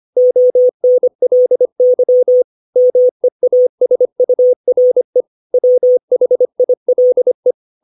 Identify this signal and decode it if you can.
Morse code, heard as a keyed tone, about 25 wpm, 500 Hz. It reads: ONLY MEASURE WHILE
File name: 632 words / 632 Words 25WPM